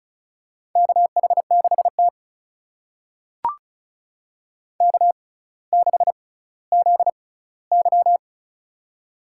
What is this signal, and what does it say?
Morse code, heard as a keyed tone, about 34 wpm, spaced as for 18 wpm, 700 Hz. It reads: KH6T K6ZY
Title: USA Call Signs - 4 Characters - 6x Character Spacing - 35wpm